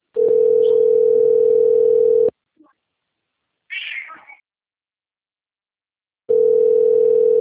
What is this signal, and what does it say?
Ringback tone (North America), 440 + 480 Hz.